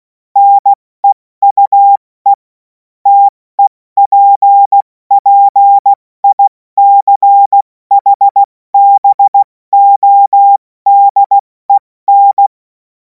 Morse code, heard as a keyed tone, about 16 wpm, spaced as for 14 wpm, 800 Hz. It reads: NEUE TEPPICHBODEN
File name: Morsekurs / Witze / Witz 10. März 2026 70 F80